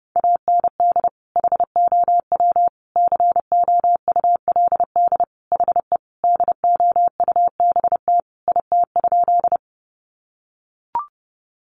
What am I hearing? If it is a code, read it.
Morse code, heard as a keyed tone, about 28 wpm, 700 Hz. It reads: AND HOW COULD HE DOUBT IT?
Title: Sentences from Top 1500 Words 30wpm